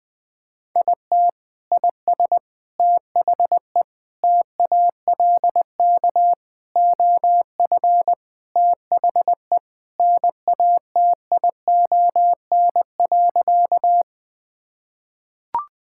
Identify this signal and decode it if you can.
Morse code, heard as a keyed tone, about 21 wpm, 700 Hz. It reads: IT IS THE TALK OF THE NATION.
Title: Sentences with 4-Word Phrases - 20wpm